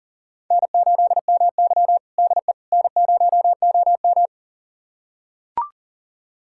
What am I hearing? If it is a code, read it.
Morse code, heard as a keyed tone, about 40 wpm, 700 Hz. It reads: N8MY DE N0OM
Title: USA Callsign DE Callsign - 4 Characters - 40wpm